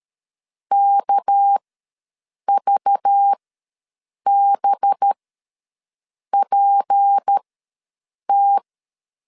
Morse code, heard as a keyed tone, about 13 wpm, spaced as for 6 wpm, 800 Hz. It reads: KVBPT
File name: LEZIONI / 6 PKX0 / 6 PKX0 ws 12wpm 3448